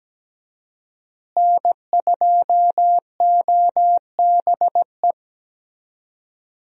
Morse code, heard as a keyed tone, about 17 wpm, 700 Hz. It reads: N2OBE
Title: USA Call Signs - 5 Characters - Repeated 3x - 17wpm